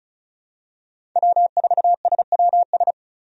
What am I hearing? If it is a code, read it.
Morse code, heard as a keyed tone, about 36 wpm, 700 Hz. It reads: W4SWS
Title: USA Call Signs - 5 Characters - 35wpm